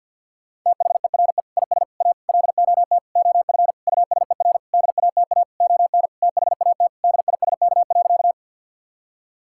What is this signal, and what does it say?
Morse code, heard as a keyed tone, about 50 wpm, 700 Hz. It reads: THERE IS A LOT OF USER DATA ON THAT DISK.
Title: Sentences from Top 1000 Words - Review Part 6 - 50wpm